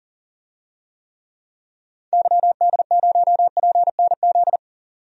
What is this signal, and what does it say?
Morse code, heard as a keyed tone, about 40 wpm, 700 Hz. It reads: YD0PNZ
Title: DXCC-100 Call Signs - 6 Characters - 40wpm